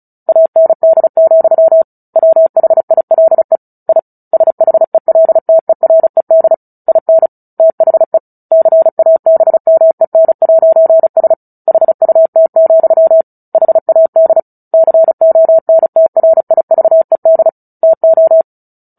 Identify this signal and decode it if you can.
Morse code, heard as a keyed tone, about 34 wpm, 650 Hz. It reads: AND, WHILE I SHELTERED IN THE CABMEN'S HUT, HAD CONTRIVED TO
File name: Book Two / 09-Chapter 9 - 35 WPM